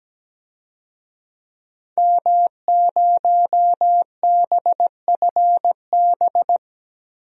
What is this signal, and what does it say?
Morse code, heard as a keyed tone, about 17 wpm, 700 Hz. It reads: M0BFB
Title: DXCC-100 Call Signs - 5 Characters - 17wpm